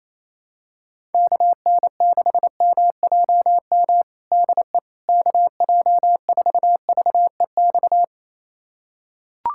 Morse code, heard as a keyed tone, about 28 wpm, 700 Hz. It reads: KN6MJM DE KJ4VEX
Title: USA Callsign DE Callsign - 6 Characters - 28wpm